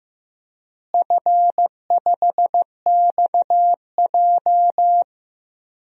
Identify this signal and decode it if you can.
Morse code, heard as a keyed tone, about 15 wpm, 700 Hz. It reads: F5XJ
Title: DXCC-100 Call Signs - 4 Characters - Repeated 2x - 15wpm